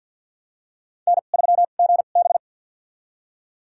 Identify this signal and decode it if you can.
Morse code, heard as a keyed tone, about 46 wpm, spaced as for 35 wpm, 700 Hz. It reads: N3GB
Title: USA Call Signs - 4 Characters - 2x Character Spacing - 50wpm